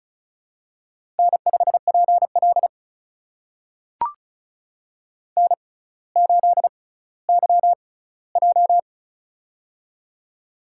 Morse code, heard as a keyed tone, about 36 wpm, spaced as for 18 wpm, 700 Hz. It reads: N5PL N8YJ
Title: USA Call Signs - 4 Characters - 6x Character Spacing - 35wpm